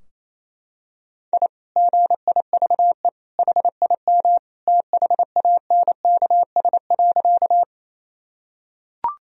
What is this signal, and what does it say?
Morse code, heard as a keyed tone, about 28 wpm, 700 Hz. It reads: I GIVE HIM THANKS.
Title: Sentences from Top 1000 Words - Review Part 3 - 28wpm